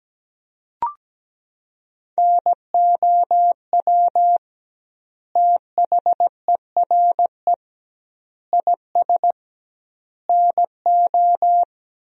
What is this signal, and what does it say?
Morse code, heard as a keyed tone, about 17 wpm, 700 Hz. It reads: NOW THERE IS NO
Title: 2x Word-Spacing - Sentences from Top 100 Words - 17wpm